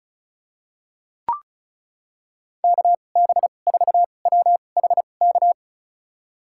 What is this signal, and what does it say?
Morse code, heard as a keyed tone, about 36 wpm, spaced as for 25 wpm, 700 Hz. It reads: KB4WHK
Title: USA Call Signs - 6 Characters - 2x Character Spacing - 35wpm